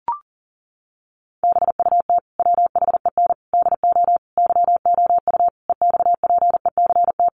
Morse code, heard as a keyed tone, about 40 wpm, 700 Hz. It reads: BUT WHEN DO YOU EXPECT
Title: Sentences from Top 1100 Words 40wpm